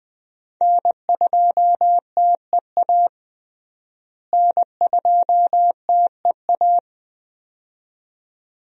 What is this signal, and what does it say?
Morse code, heard as a keyed tone, about 20 wpm, 700 Hz. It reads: N2TEA N2TEA
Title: USA Call Signs - 5 Characters - Repeated 2x - 20wpm